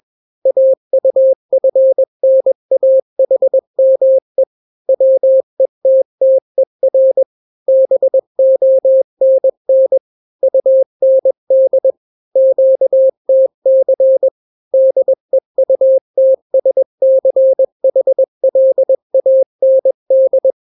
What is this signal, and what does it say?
Morse code, heard as a keyed tone, about 21 wpm, 530 Hz. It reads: AUFNAHME WETTER BONN UND QTC DEUTSCHLAND
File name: daily / 20wpm